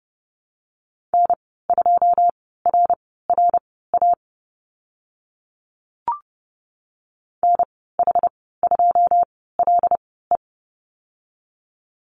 Morse code, heard as a keyed tone, about 30 wpm, spaced as for 17 wpm, 700 Hz. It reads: N2RRA NH2LE
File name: Lesson 14g - Number 2 - Call Signs 30wpm